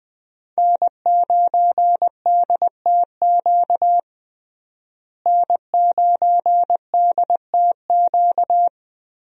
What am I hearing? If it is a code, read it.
Morse code, heard as a keyed tone, about 20 wpm, 700 Hz. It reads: N9DTQ N9DTQ
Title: USA Call Signs - 5 Characters - Repeated 3x - 20wpm